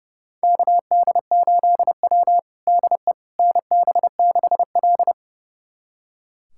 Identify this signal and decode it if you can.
Morse code, heard as a keyed tone, about 30 wpm, 700 Hz. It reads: KD8W DE NB6L
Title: USA Callsign DE Callsign - 4 Characters - 30wpm